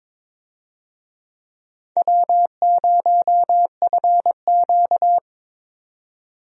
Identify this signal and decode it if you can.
Morse code, heard as a keyed tone, about 22 wpm, 700 Hz. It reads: W0FQ
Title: USA Call Signs - 4 Characters - 22wpm